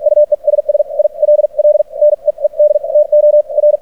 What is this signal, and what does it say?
Morse code, heard as a keyed tone, about 52 wpm, spaced as for 41 wpm, 600 Hz. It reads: UESS ILL NEED TO R
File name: archived / CRSnet